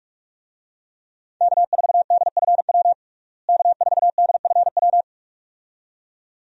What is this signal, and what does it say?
Morse code, heard as a keyed tone, about 42 wpm, 700 Hz. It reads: K4DFW K4DFW
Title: USA Call Signs - 5 Characters - Repeated 2x - 45wpm